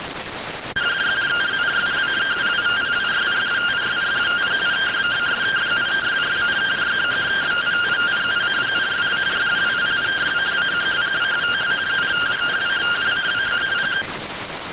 Two-tone teleprinter signal, about 170 Hz shift.